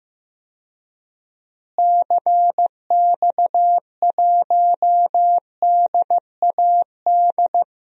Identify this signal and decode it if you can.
Morse code, heard as a keyed tone, about 15 wpm, 700 Hz. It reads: CX1DAD
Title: DXCC-100 Call Signs - 6 Characters - Repeated 2x - 15wpm